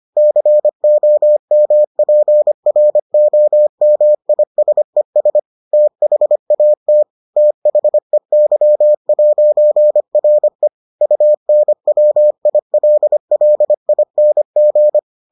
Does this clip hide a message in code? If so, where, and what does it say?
Yes: Morse code from 0.2 to 15.0 s, heard as a keyed tone, about 25 wpm, 600 Hz. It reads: COMPROMISES THAT THEY'RE UNWILLING